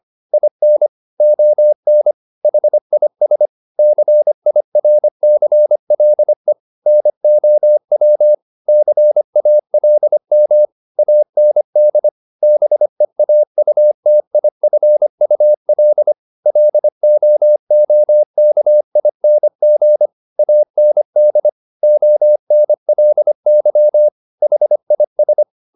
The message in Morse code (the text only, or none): IN ON HIS CIRCLE NOW CALM AND BEAUTIFUL LOOKING AND ONLY HIS